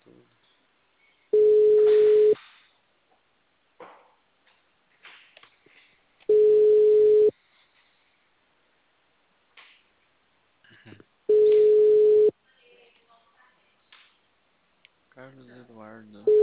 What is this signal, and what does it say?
Ringback tone (Europe, 425 Hz), with speech.